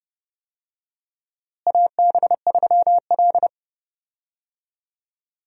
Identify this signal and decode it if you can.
Morse code, heard as a keyed tone, about 31 wpm, 700 Hz. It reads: AB3L
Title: USA Call Signs - 4 Characters - Repeated 2x - 30wpm